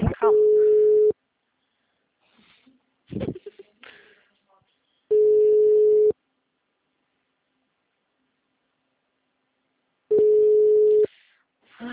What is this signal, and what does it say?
Ringback tone (Europe, 425 Hz), with speech.
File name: out- -2024- - - .93341